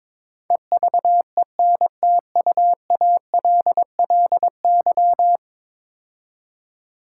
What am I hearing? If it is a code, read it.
Morse code, heard as a keyed tone, about 22 wpm, 700 Hz. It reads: EVENTUALLY
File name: practice / 22wpm / 10 Letter Words - First 3 Letters Given - 22wpm